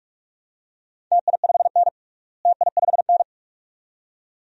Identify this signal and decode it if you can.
Morse code, heard as a keyed tone, about 41 wpm, 700 Hz. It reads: TI5N TI5N
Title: DXCC-100 Call Signs - 4 Characters - Repeated 2x - 45wpm